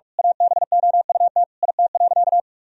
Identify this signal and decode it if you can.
Morse code, heard as a keyed tone, about 41 wpm, 700 Hz. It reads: ABOUT IT.
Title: Sentences from Top 1000 Words - Review Part 3 - 45wpm